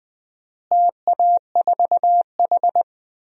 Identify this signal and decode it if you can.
Morse code, heard as a keyed tone, about 20 wpm, 700 Hz. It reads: TA4H
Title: DXCC-100 Call Signs - 4 Characters - Repeated 3x - 20wpm